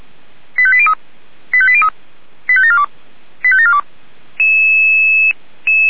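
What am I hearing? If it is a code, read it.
Five-tone selective call (ZVEI), heard as quick runs of five tones, tones of about 70 ms. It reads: 85803 85803 85742 85742
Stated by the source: ZVEI five-tone selective call 85719, received by SDR.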